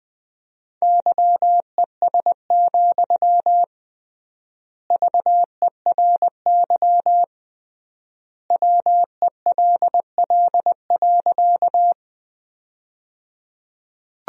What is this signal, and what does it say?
Morse code, heard as a keyed tone, about 20 wpm, 700 Hz. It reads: YES, VERY WELL.
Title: 3x Word-Spacing - Sentences from Top 500 Words - 20wpm